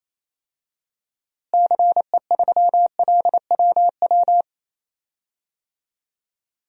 Morse code, heard as a keyed tone, about 28 wpm, 700 Hz. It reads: CE3LWW
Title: DXCC-100 Call Signs - 6 Characters - Repeated 3x - 28wpm